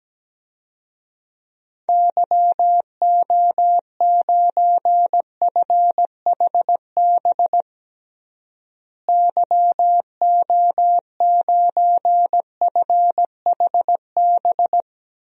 Morse code, heard as a keyed tone, about 17 wpm, 700 Hz. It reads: YO9FHB YO9FHB
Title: DXCC-100 Call Signs - 6 Characters - Repeated 2x - 17wpm